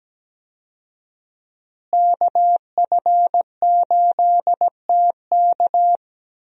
Morse code, heard as a keyed tone, about 17 wpm, 700 Hz. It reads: KF8TK
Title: USA Call Signs - 5 Characters - Repeated 3x - 17wpm